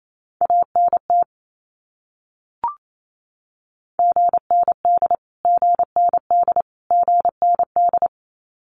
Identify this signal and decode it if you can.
Morse code, heard as a keyed tone, about 28 wpm, 700 Hz. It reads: ANT GND GND GND
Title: Abbreviations Repeated 3x - 28wpm